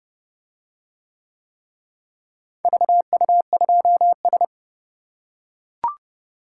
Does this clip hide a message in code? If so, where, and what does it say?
Yes: Morse code from 2.6 to 4.4 s, heard as a keyed tone, about 30 wpm, 700 Hz. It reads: VU2S